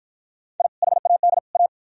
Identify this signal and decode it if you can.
Morse code, heard as a keyed tone, about 50 wpm, 700 Hz. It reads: I HAD A